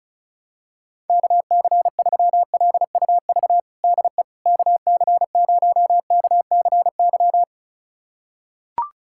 Morse code, heard as a keyed tone, about 36 wpm, 700 Hz. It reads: KC3LUV DE KC0KCY
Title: USA Callsign DE Callsign 35wpm